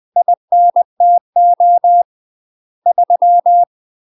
Morse code, heard as a keyed tone, about 20 wpm, 700 Hz. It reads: INTO 3
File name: UK MASTER QSOs TRAINING / 20WPM / UK MASTER QSOs 0067